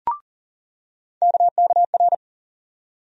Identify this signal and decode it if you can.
Morse code, heard as a keyed tone, about 40 wpm, 700 Hz. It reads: KKR
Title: USA Call Sign Postfixes - 3 Letters 40wpm